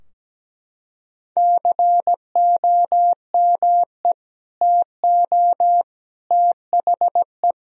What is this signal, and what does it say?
Morse code, heard as a keyed tone, about 17 wpm, 700 Hz. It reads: COME TO THE
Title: Sentences from Top 1000 Words - Review Part 2 - 17wpm